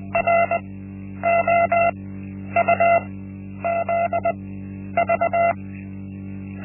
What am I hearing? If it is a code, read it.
Morse code, heard as a keyed tone, about 20 wpm, spaced as for 10 wpm, 680 Hz. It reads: ROUZV
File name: Linksmakalnis-1